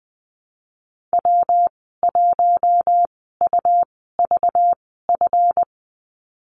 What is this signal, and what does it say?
Morse code, heard as a keyed tone, about 20 wpm, spaced as for 15 wpm, 700 Hz. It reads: W1UVF
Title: 231 Lesson 37g - Letter x - Call Signs 20wpm